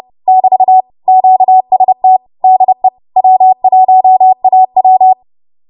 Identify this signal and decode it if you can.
Morse code, heard as a keyed tone, about 30 wpm, 750 Hz. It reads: <BT> QST DE W1AW
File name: Morse / 30WPM